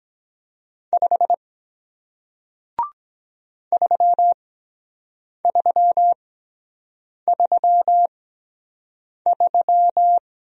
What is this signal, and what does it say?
Morse code, heard as a keyed tone, about 23 wpm, spaced as for 7 wpm, 700 Hz. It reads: 5 3333